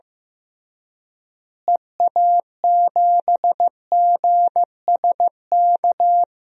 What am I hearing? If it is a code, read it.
Morse code, heard as a keyed tone, about 15 wpm, 700 Hz. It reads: EA7GSK
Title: DXCC-100 Call Signs - 6 Characters - Repeated 3x - 15wpm